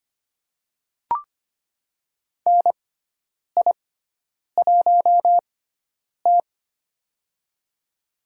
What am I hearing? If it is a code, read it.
Morse code, heard as a keyed tone, about 25 wpm, spaced as for 9 wpm, 700 Hz. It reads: NI1T
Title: USA Call Signs - 6x Character Spacing - 25wpm